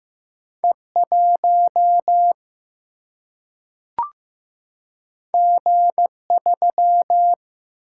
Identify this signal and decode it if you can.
Morse code, heard as a keyed tone, about 15 wpm, 700 Hz. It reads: E 1 G 3
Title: Single-Letter Single-Number 15wpm